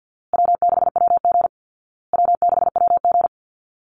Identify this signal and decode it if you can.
Morse code, heard as a keyed tone, about 48 wpm, 700 Hz. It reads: W6WG W6WG
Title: Easy Call Signs Repeated 3x - 50wpm